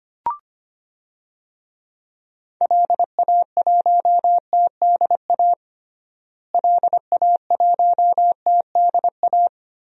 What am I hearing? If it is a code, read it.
Morse code, heard as a keyed tone, about 25 wpm, 700 Hz. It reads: LA1TDA LA1TDA
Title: DXCC-100 Call Signs - 6 Characters - Repeated 2x - 25wpm